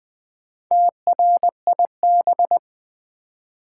Morse code, heard as a keyed tone, about 20 wpm, 690 Hz. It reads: TRIB
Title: Build A Word - 2 Letters At A Time - 6-Letter Words - 20wpm